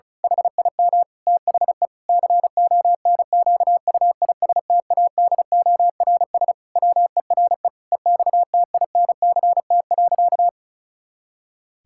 Morse code, heard as a keyed tone, about 34 wpm, 700 Hz. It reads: HIM THE CONQUISTADORS WERE EXTINCT.
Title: Pangram Challenge v2 - 35wpm